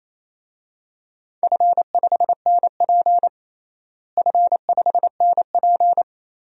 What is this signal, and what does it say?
Morse code, heard as a keyed tone, about 28 wpm, 700 Hz. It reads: F5NP F5NP
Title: DXCC-100 Call Signs - 4 Characters - Repeated 3x - 28wpm